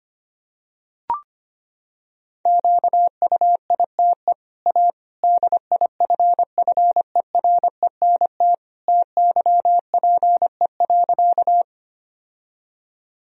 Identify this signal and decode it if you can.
Morse code, heard as a keyed tone, about 25 wpm, 700 Hz. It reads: QUITE A DIFFERENT TYPE.
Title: Sentences from Top 1000 Words - Review Part 3 - 25wpm